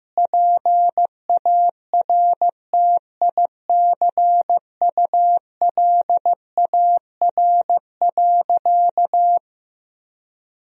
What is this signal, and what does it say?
Morse code, heard as a keyed tone, about 15 wpm, 700 Hz. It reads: PARTICULAR.
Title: Sentences from Top 1000 Words - Review Part 4 - 15wpm